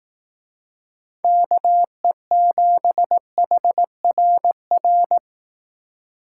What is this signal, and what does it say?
Morse code, heard as a keyed tone, about 18 wpm, 700 Hz. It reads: KE7HRR
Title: Speed Racing - USA 2x3 Call Signs 15wpm